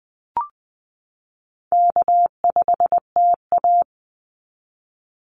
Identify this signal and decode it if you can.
Morse code, heard as a keyed tone, about 20 wpm, 700 Hz. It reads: K5TA
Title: Easy Call Signs - 20wpm